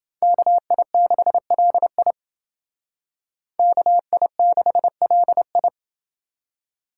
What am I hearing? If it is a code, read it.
Morse code, heard as a keyed tone, about 28 wpm, 700 Hz. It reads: KI6LI KI6LI